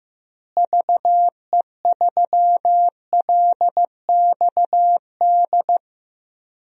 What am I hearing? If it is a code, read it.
Morse code, heard as a keyed tone, about 15 wpm, 700 Hz. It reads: VE3LXD
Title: DXCC-100 Call Signs - 6 Characters - Repeated 3x - 15wpm